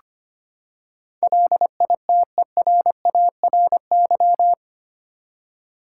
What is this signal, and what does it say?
Morse code, heard as a keyed tone, about 25 wpm, 700 Hz. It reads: LITERARY
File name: Eight Letter Words - 25wpm